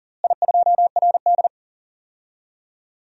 Morse code, heard as a keyed tone, about 37 wpm, 700 Hz. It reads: I2RD
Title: DXCC-100 Call Signs - 4 Characters - Repeated 3x - 40wpm